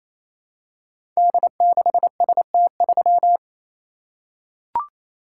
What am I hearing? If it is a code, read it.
Morse code, heard as a keyed tone, about 27 wpm, 700 Hz. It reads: D6ST3
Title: Code Groups - 5 Letter:Numbers - 28wpm